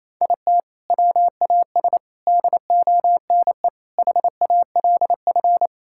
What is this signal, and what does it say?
Morse code, heard as a keyed tone, about 27 wpm, 700 Hz. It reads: IT WAS DONE HALF
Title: Sentences from Top 1000 Words - Review Part 5 - 28wpm